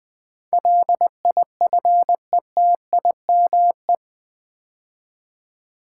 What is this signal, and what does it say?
Morse code, heard as a keyed tone, about 20 wpm, 700 Hz. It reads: LIFETIME
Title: 8 Letter Words - First 3 Letters Given - 20wpm